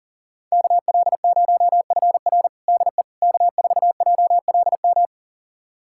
Morse code, heard as a keyed tone, about 40 wpm, 700 Hz. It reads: KL0FR DE K4JLM
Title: USA Callsign DE Callsign - 5 Characters - 40wpm